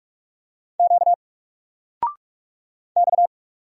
Morse code, heard as a keyed tone, about 46 wpm, 700 Hz. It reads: Q X
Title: Single Letter-Number 45wpm